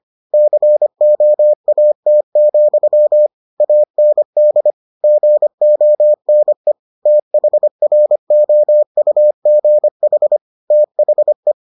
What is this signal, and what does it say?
Morse code, heard as a keyed tone, about 25 wpm, 600 Hz. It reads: COAT, AND GONE THROUGH THE